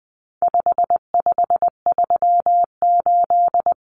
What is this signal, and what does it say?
Morse code, heard as a keyed tone, about 20 wpm, 700 Hz. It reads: 5538
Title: Common Cut Numbers - 4-Digit 20wpm